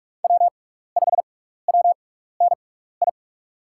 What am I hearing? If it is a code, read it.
Morse code, heard as a keyed tone, about 44 wpm, spaced as for 15 wpm, 700 Hz. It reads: W5WNI
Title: USA Call Signs - 5 Characters - 6x Character Spacing - 45wpm